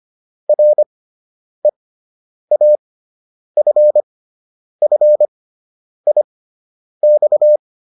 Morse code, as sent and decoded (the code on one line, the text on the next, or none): .-. .- ..-. ..-. .. -..-
REAFFIX